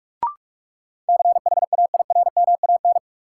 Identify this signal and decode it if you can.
Morse code, heard as a keyed tone, about 41 wpm, 700 Hz. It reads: CHAIRMAN
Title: Sets of 3 Words - New Word Drill - 45wpm